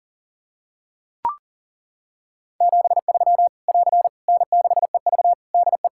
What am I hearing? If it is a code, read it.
Morse code, heard as a keyed tone, about 37 wpm, 700 Hz. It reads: 73 <AR> N6EV DE
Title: K1USN SST QSO Practice 40wpm